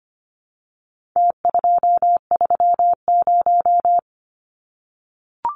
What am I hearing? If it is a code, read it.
Morse code, heard as a keyed tone, about 25 wpm, 700 Hz. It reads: T230
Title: Numbers - Contest Style - 4-Digit 25wpm